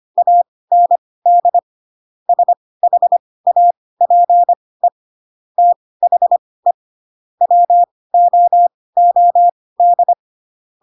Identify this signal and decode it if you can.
Morse code, heard as a keyed tone, about 25 wpm, spaced as for 18 wpm, 700 Hz. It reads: AND SHAPE THE WOOD
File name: Pinocchio ch3 12 25 ik1hge